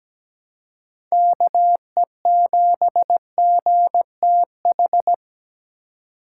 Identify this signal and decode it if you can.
Morse code, heard as a keyed tone, about 17 wpm, 700 Hz. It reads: KE7GTH
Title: USA Call Signs - 6 Characters - 4x Character Spacing - 17wpm